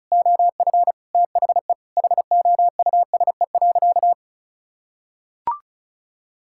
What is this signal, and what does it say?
Morse code, heard as a keyed tone, about 36 wpm, 700 Hz. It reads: OF THE HOUSE.
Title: Sentences from Top 1600 Words 35wpm